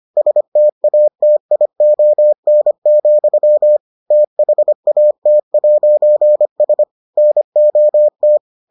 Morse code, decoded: STATION, THAT'S NOT

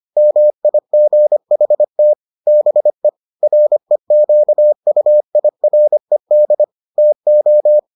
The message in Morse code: MIGHT BE REQUIRED TO

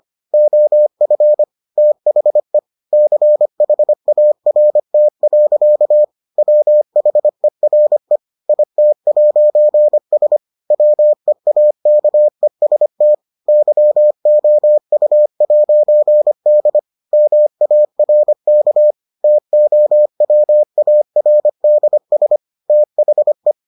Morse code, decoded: OF THE CHART. WHERE IT'S WEAKEST YOU'D MARK TOWARDS THE

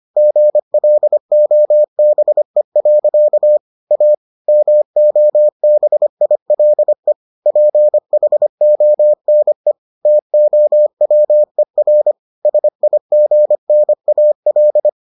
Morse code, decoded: GLOBE. A MOBILE PHONE TOWER SIGNAL

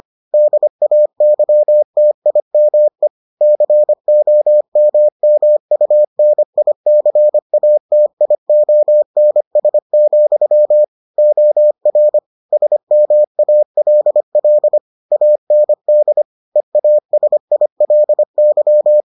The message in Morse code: DAYTIME COMMUNICATIONS, OR SMALL AND EASILY

